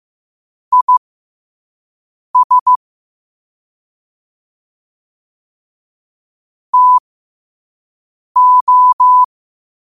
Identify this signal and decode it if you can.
Morse code, heard as a keyed tone, about 15 wpm, spaced as for 5 wpm, 1000 Hz. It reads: IS TO
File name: qotdfiles / 5wpm